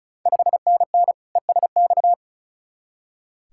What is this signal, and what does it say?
Morse code, heard as a keyed tone, about 36 wpm, 690 Hz. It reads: 5NN ESX
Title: Ontario QSO Party Exchange 35wpm